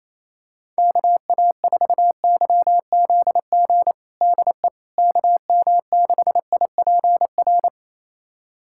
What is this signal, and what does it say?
Morse code, heard as a keyed tone, about 28 wpm, 700 Hz. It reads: KA4YZG DE KM6IPR
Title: USA Callsign DE Callsign - 6 Characters - 28wpm